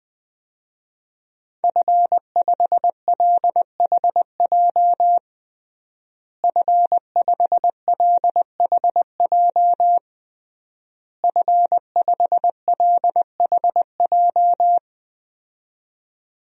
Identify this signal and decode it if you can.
Morse code, heard as a keyed tone, about 21 wpm, 700 Hz. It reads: F5LHJ F5LHJ F5LHJ
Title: DXCC-100 Call Signs - 5 Characters - Repeated 3x - 20wpm